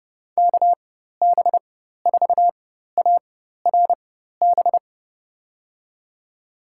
Morse code, heard as a keyed tone, about 30 wpm, spaced as for 14 wpm, 700 Hz. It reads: KB4ARB
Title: USA Call Signs - 6 Characters - 4x Character Spacing - 30wpm